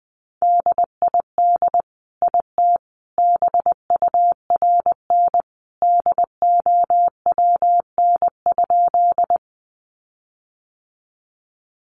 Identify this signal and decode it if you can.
Morse code, heard as a keyed tone, about 20 wpm, 700 Hz. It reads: DID IT BURN DOWN?
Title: Sentences from Top 1300 Words 20wpm